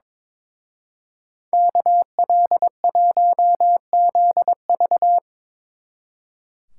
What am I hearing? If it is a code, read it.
Morse code, heard as a keyed tone, about 25 wpm, 700 Hz. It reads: KL1ZV
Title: Speed Racing - USA 2x2 Call Signs 22wpm